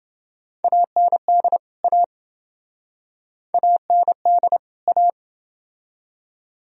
Morse code, heard as a keyed tone, about 25 wpm, 700 Hz. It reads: AND A AND A
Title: Speed-Racing - Sets of 2 Words 20wpm